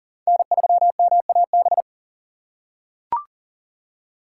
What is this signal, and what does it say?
Morse code, heard as a keyed tone, about 40 wpm, 700 Hz. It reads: N3MAB
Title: USA Call Signs - 5 Characters - 6x Character Spacing - 40wpm